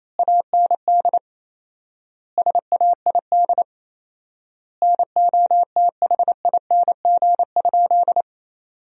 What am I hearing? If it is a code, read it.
Morse code, heard as a keyed tone, about 28 wpm, 700 Hz. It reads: AND SAID NOTHING?
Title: 4x Word-Spacing - Sentences from Top 500 Words - 28wpm